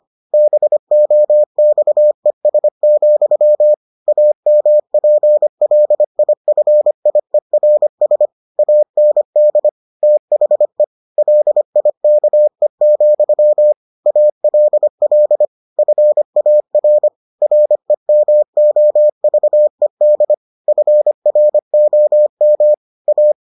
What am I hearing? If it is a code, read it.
Morse code, heard as a keyed tone, about 25 wpm, 600 Hz. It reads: BOXES, AMPLIFIERS AND THE LIKE, ALL FAR REMOVED FROM A